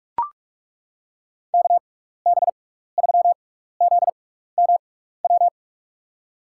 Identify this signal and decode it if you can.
Morse code, heard as a keyed tone, about 42 wpm, spaced as for 15 wpm, 700 Hz. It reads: KB3ZMW
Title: USA Call Signs - 6 Characters - 6x Character Spacing - 45wpm